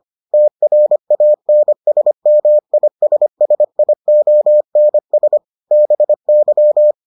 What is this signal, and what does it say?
Morse code, heard as a keyed tone, about 25 wpm, 600 Hz. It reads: TRANSMISSIONS BY